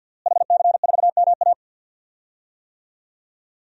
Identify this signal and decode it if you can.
Morse code, heard as a keyed tone, about 52 wpm, 700 Hz. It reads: HC4MA
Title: DXCC-100 Call Signs - 5 Characters - Repeated 2x - 50wpm